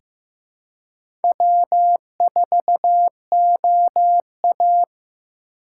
Morse code, heard as a keyed tone, about 16 wpm, 700 Hz. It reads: W4OA
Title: USA Call Signs - 4 Characters - 3x Character Spacing - 15wpm